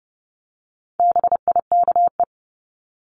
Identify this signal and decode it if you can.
Morse code, heard as a keyed tone, about 30 wpm, 700 Hz. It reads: BIKE
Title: Top 1751-2000 Words 30wpm